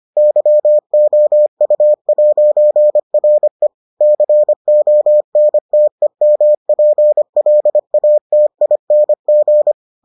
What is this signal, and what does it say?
Morse code, heard as a keyed tone, about 25 wpm, 600 Hz. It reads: YOU'RE CONTEMPLATING